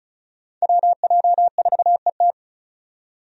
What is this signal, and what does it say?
Morse code, heard as a keyed tone, about 36 wpm, 700 Hz. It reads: WJ4ET
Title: USA Call Signs - 5 Characters - 35wpm